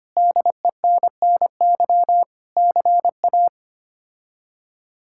Morse code, heard as a keyed tone, about 25 wpm, 700 Hz. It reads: DENNY CA
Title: K1USN SST QSO Practice - 25wpm